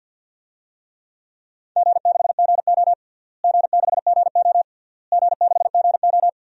Morse code, heard as a keyed tone, about 49 wpm, 700 Hz. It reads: G6GO G6GO G6GO
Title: DXCC-100 Call Signs - 4 Characters - Repeated 3x - 50wpm